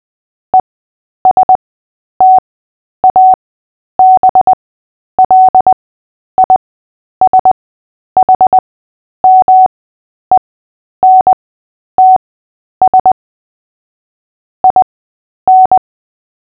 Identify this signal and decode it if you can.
Morse code, heard as a keyed tone, about 20 wpm, spaced as for 10 wpm, 750 Hz. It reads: ESTABLISHMENTS IN
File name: Short-Story-7-10wpm